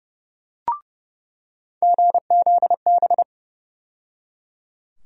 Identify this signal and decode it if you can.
Morse code, heard as a keyed tone, about 30 wpm, 700 Hz. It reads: GZB
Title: USA Call Sign Postfixes - 3 Letters 30wpm